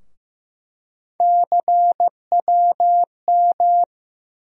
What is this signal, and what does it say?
Morse code, heard as a keyed tone, about 15 wpm, 700 Hz. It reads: CWM
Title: USA Call Sign Postfixes - 3 Letters 15wpm